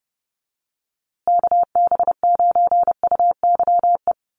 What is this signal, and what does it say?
Morse code, heard as a keyed tone, about 30 wpm, 700 Hz. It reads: KB9UYE
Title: USA Call Signs Repeated 3x 30wpm